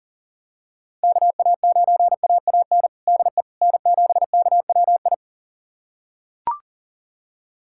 Morse code, heard as a keyed tone, about 40 wpm, 700 Hz. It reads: KA9AAN DE N7KWI
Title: USA Callsign DE Callsign 40wpm